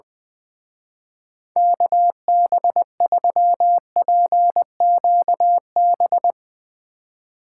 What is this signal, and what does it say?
Morse code, heard as a keyed tone, about 22 wpm, 700 Hz. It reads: KB3PQB
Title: Speed Racing - USA 2x3 Call Signs 20wpm